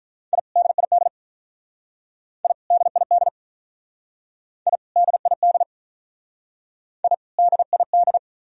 Morse code, heard as a keyed tone, about 43 wpm, 700 Hz. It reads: I DID I DID I DID I DID
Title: Speed-Racing - Sets of 2 Words 35wpm